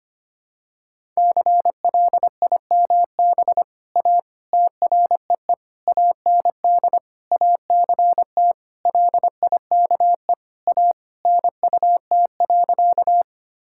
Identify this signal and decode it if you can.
Morse code, heard as a keyed tone, about 25 wpm, 700 Hz. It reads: CLIMB A TREE AND ACT LIKE A NUT.
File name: practice / 25wpm / All-Age Appropriate Jokes - 25wpm